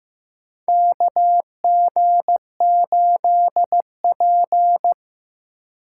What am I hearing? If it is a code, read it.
Morse code, heard as a keyed tone, about 15 wpm, 700 Hz. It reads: KG8P
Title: USA Call Signs - 4 Characters - 2x Character Spacing - 15wpm